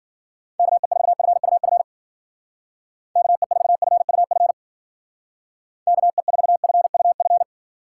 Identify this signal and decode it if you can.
Morse code, heard as a keyed tone, about 48 wpm, 700 Hz. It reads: KE4FUF KE4FUF KE4FUF
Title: Speed Racing - USA Call Signs 40wpm